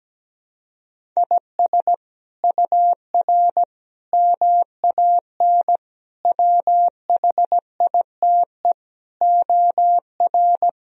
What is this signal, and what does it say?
Morse code, heard as a keyed tone, about 17 wpm, 700 Hz. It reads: IS UR MAN WHITE OR
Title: Sentences from Top 400 Words and Abbreviations 17wpm